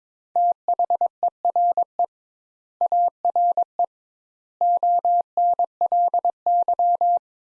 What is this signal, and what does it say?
Morse code, heard as a keyed tone, about 21 wpm, 700 Hz. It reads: THERE ARE ONLY
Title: Sentences Top 200 Words - 22wpm